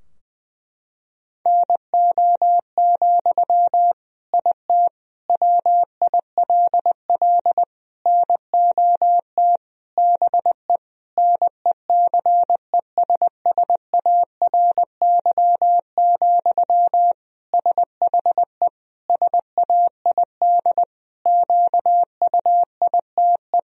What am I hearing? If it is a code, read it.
Morse code, heard as a keyed tone, about 20 wpm, 700 Hz. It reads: NO, IT WILL NOT BE NECESSARY, SHE SAID QUITE
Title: Sentences from Top 1000 Words - Review Part 4 - 20wpm